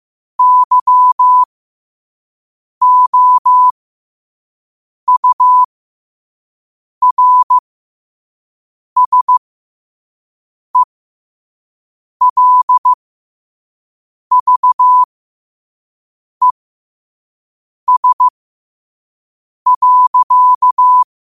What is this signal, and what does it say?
Morse code, heard as a keyed tone, about 15 wpm, spaced as for 5 wpm, 1000 Hz. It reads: YOURSELVES.
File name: qotdfiles / 5wpm